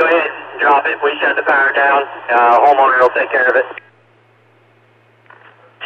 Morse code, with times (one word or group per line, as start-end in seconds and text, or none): none